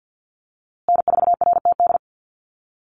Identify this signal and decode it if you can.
Morse code, heard as a keyed tone, about 48 wpm, 700 Hz. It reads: N4RTD
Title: USA Call Signs Repeated 3x 50wpm